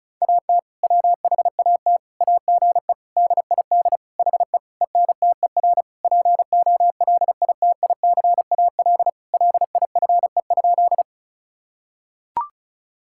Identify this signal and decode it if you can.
Morse code, heard as a keyed tone, about 36 wpm, 700 Hz. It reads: AT WHAT AGE DID HE ENTER POLITICAL LIFE?
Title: Sentences from Top 1000 Words - Review Part 5 - 35wpm